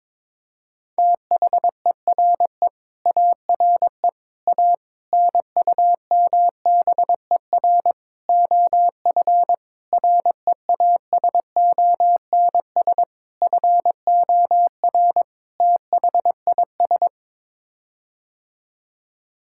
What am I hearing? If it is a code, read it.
Morse code, heard as a keyed tone, about 22 wpm, 700 Hz. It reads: THERE ARE A NUMBER OF REASONS FOR THIS
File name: practice / 22wpm / Sets of 8 Words - 22wpm